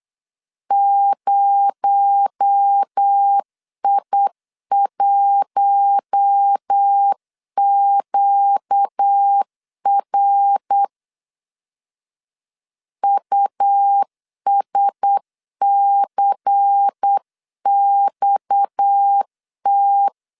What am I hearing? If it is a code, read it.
Morse code, heard as a keyed tone, about 8 wpm, 800 Hz. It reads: 0I1QR USCXT
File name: 8wpm